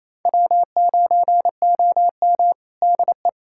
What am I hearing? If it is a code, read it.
Morse code, heard as a keyed tone, about 28 wpm, 700 Hz. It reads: W9OM DE
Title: USA Callsign DE Callsign - 4 Characters - 28wpm